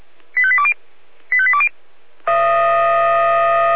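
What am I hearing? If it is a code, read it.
Five-tone selective call (ZVEI), heard as quick runs of five tones, tones of about 70 ms. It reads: 85820 85820
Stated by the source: ZVEI five-tone selective call received with an SDR.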